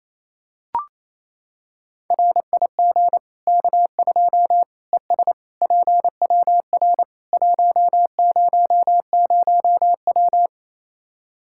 Morse code, heard as a keyed tone, about 28 wpm, 700 Hz. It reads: RIG K2 ES PWR 100W
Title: Practice QSO - Exchange 2 - Info - 28wpm